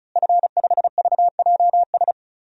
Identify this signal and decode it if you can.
Morse code, heard as a keyed tone, about 36 wpm, 700 Hz. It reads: F5VJS
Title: DXCC-100 Call Signs - 5 Characters - Repeated 2x - 35wpm